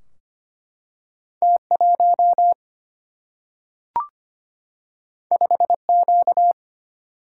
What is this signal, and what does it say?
Morse code, heard as a keyed tone, about 25 wpm, 700 Hz. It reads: T1 5Q
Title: Single-Letter Single-Number 25wpm